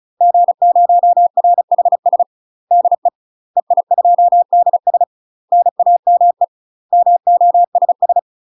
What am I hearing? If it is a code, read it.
Morse code, heard as a keyed tone, about 34 wpm, 700 Hz. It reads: G0RHS DE EI2DS NAME MOSS
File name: UK MASTER QSOs TRAINING / 35WPM / UK MASTER QSOs 0000